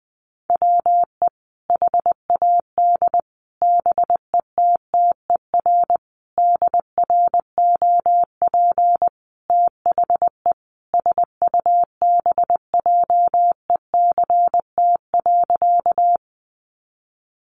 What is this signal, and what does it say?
Morse code, heard as a keyed tone, about 20 wpm, 700 Hz. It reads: WE HAD BETTER DROP THE SUBJECT.
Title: Sentences from Top 1400 Words 20wpm